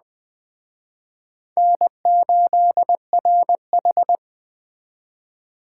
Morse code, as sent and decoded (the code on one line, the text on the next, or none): -. ---.. .-. ....
N8RH